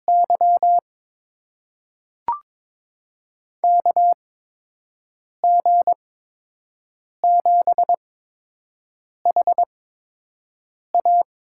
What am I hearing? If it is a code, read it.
Morse code, heard as a keyed tone, about 22 wpm, spaced as for 6 wpm, 700 Hz. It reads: Y KG7HA
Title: USA Call Signs - 5 Characters - 8x Character Spacing - 22wpm